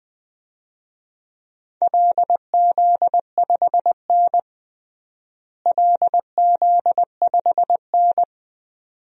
Morse code, heard as a keyed tone, about 20 wpm, 700 Hz. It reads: LZ5N LZ5N
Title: DXCC-100 Call Signs - 4 Characters - Repeated 2x - 20wpm